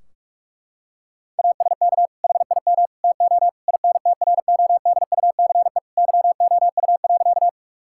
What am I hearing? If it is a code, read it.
Morse code, heard as a keyed tone, about 46 wpm, 700 Hz. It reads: ASK HIM TO INTRODUCE YOU.
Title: Sentences from Top 1600 Words 45wpm